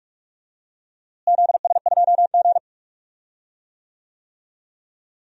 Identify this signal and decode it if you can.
Morse code, heard as a keyed tone, about 42 wpm, 700 Hz. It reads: ZS2G
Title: DXCC-100 Call Signs - 4 Characters - Repeated 2x - 45wpm